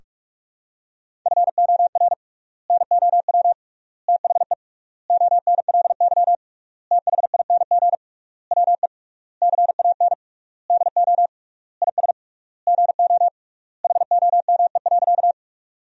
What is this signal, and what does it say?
Morse code, heard as a keyed tone, about 46 wpm, 700 Hz. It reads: FOR NOW THE ONLY THING WE CAN DO IS GO HOME.
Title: 3x Word-Spacing - Sentences from Top 200 Words - 45wpm